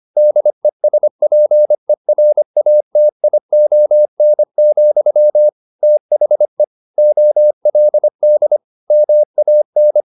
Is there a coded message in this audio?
Yes: Morse code from 0.2 to 10.0 s, heard as a keyed tone, about 25 wpm, 600 Hz. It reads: DESPERATION, THE OLD MAN